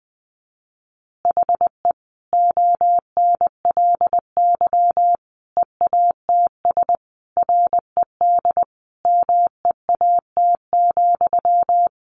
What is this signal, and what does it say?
Morse code, heard as a keyed tone, about 20 wpm, 700 Hz. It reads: HE ONLY EATS RED MEAT,
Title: Easy Phrases 20wpm